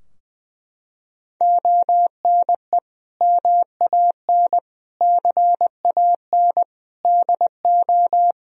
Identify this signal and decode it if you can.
Morse code, heard as a keyed tone, about 21 wpm, 700 Hz. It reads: ONE MAN CAN DO
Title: Sentences from Top 1000 Words - Review Part 4 - 20wpm